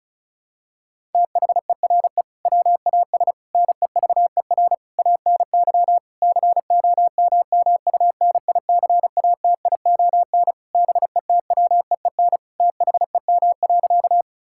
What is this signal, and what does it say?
Morse code, heard as a keyed tone, about 36 wpm, 700 Hz. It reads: THERE WAS NEVER ANY COMMUNICATION BETWEEN THEM.
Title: Sentences from Top 1000 Words - Review Part 6 - 35wpm